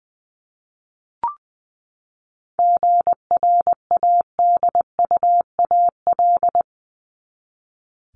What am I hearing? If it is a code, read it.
Morse code, heard as a keyed tone, about 20 wpm, 700 Hz. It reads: GRADUAL
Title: LY Suffix - Top 100 Words with Base Words 20wpm